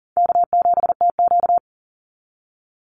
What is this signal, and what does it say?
Morse code, heard as a keyed tone, about 40 wpm, 700 Hz. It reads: K7TQ
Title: NA SOTA Chaser Callsigns 40wpm